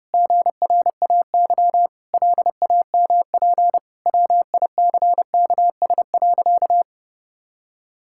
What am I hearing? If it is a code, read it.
Morse code, heard as a keyed tone, about 30 wpm, 700 Hz. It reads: GRAY LAMP WICKS.
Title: Pangram Challenge v2 - 30wpm